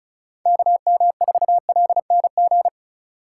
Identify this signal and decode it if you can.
Morse code, heard as a keyed tone, about 36 wpm, 700 Hz. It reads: KM4LNG
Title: USA Call Signs - 6 Characters - 3x Character Spacing - 35wpm